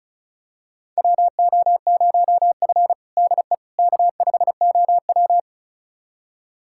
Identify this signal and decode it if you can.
Morse code, heard as a keyed tone, about 36 wpm, 700 Hz. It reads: WO0F DE K5OW
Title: USA Callsign DE Callsign - 4 Characters - 35wpm